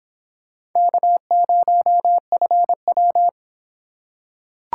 Morse code, heard as a keyed tone, about 25 wpm, 700 Hz. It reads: K0FW